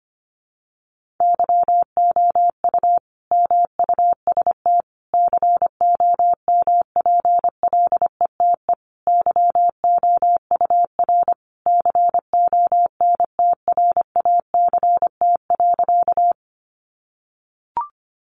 Morse code, heard as a keyed tone, about 25 wpm, 700 Hz. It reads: YOU MUST COMPLETE YOUR CONTRACT.
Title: Sentences from Top 1400 Words 25wpm